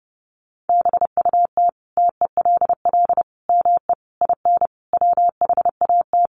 Morse code, heard as a keyed tone, about 30 wpm, 700 Hz. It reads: BUT TELL ME IN WHAT
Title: Sentences from Top 1300 Words 30wpm